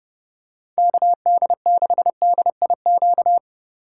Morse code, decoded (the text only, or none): KD6DIQ